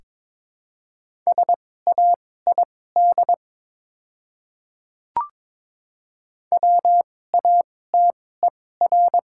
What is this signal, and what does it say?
Morse code, heard as a keyed tone, about 21 wpm, spaced as for 16 wpm, 700 Hz. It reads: SAID WATER
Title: 2x Character Spacing - Top 100 Words - 22wpm